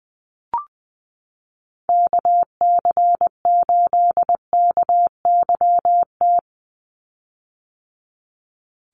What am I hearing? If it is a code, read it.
Morse code, heard as a keyed tone, about 20 wpm, 700 Hz. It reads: KC8KYT
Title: USA Call Signs 20wpm